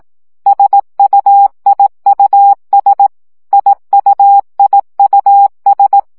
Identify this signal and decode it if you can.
Morse code, heard as a keyed tone, about 18 wpm, 800 Hz. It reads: SUIUS IUIUS
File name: CW gyakorlas i s u 15 baud